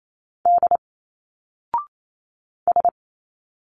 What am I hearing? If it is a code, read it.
Morse code, heard as a keyed tone, about 28 wpm, 700 Hz. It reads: D S
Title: Single Letters 28wpm